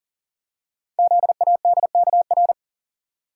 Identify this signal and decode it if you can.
Morse code, heard as a keyed tone, about 42 wpm, 690 Hz. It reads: ZADKR
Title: Code Groups - 5 Letters - 40wpm